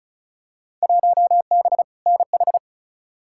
Morse code, heard as a keyed tone, about 36 wpm, 700 Hz. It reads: 1B NH
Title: Field Day Exchanges Encore - 35wpm